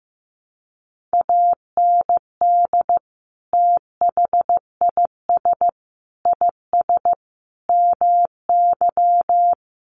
Morse code, heard as a keyed tone, about 15 wpm, 700 Hz. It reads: AND THIS IS MY
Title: Sentences from Top 1200 Words 15wpm